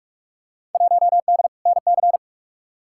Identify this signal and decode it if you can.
Morse code, heard as a keyed tone, about 44 wpm, 700 Hz. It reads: 1D NC
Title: Field Day Exchanges Encore - 45wpm